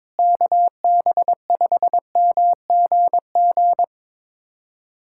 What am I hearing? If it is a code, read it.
Morse code, heard as a keyed tone, about 22 wpm, 700 Hz. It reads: KB5MGG
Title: USA Callsign DE Callsign 22wpm